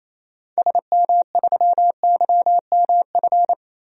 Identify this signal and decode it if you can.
Morse code, heard as a keyed tone, about 28 wpm, 700 Hz. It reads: SM3YMF
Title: DXCC-100 Call Signs - 6 Characters - Repeated 2x - 28wpm